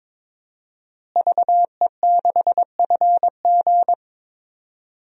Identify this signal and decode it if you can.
Morse code, heard as a keyed tone, about 22 wpm, 700 Hz. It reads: VE6FG
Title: DXCC-100 Call Signs - 5 Characters - Repeated 3x - 22wpm